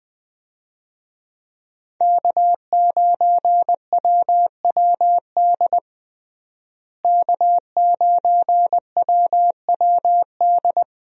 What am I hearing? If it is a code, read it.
Morse code, heard as a keyed tone, about 20 wpm, 700 Hz. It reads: K9WWD K9WWD
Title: USA Call Signs - 5 Characters - Repeated 3x - 20wpm